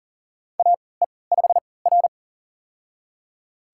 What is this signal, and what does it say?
Morse code, heard as a keyed tone, about 39 wpm, spaced as for 22 wpm, 700 Hz. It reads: AE5R
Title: USA Call Signs - 4 Characters - 3x Character Spacing - 40wpm